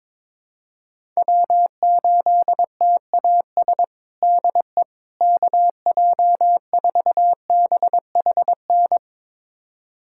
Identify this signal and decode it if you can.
Morse code, heard as a keyed tone, about 21 wpm, 700 Hz. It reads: W8TAS DE KJ4BHN
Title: USA Callsign DE Callsign 22wpm